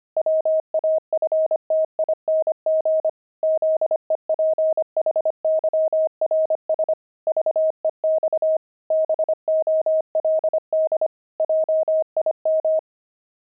Morse code, heard as a keyed tone, about 25 wpm, 630 Hz. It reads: WAFTING ZEPHYRS VEX BOLD JIM
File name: pangram-fs-25-cs-25-zephyrs